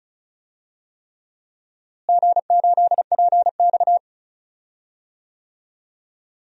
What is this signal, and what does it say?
Morse code, heard as a keyed tone, about 34 wpm, 700 Hz. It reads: G8PX
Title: DXCC-100 Call Signs - 4 Characters - Repeated 3x - 35wpm